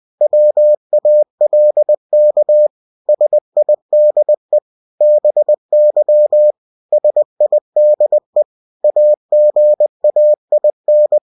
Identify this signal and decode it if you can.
Morse code, heard as a keyed tone, about 20 wpm, 600 Hz. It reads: WALK SIDE BY SIDE AGAIN
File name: cw-books / winnie-the-pooh / winnie-20wpm-page-0034